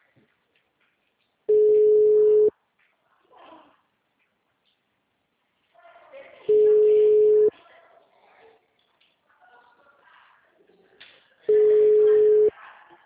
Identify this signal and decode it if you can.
Ringback tone (Europe, 425 Hz), with speech.
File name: out- -2017- - - .56